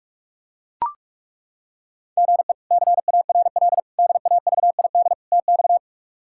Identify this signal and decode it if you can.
Morse code, heard as a keyed tone, about 41 wpm, 700 Hz. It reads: GE CARL DAVID TX
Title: K1USN SST QSO Practice - 45wpm